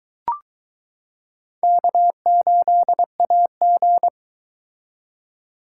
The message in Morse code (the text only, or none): K8AG